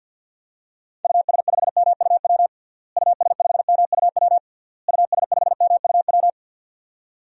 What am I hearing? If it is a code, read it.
Morse code, heard as a keyed tone, about 52 wpm, 700 Hz. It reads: US5MUW US5MUW US5MUW
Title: DXCC-100 Call Signs - 6 Characters - Repeated 3x - 50wpm